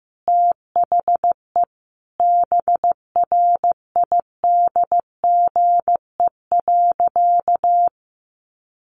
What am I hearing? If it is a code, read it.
Morse code, heard as a keyed tone, about 15 wpm, 700 Hz. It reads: THE BRIDGE.
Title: Sentences from Top 1200 Words 15wpm